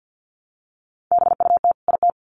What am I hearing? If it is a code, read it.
Morse code, heard as a keyed tone, about 50 wpm, 700 Hz. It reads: BUT IT
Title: Sets of 2 Words 50wpm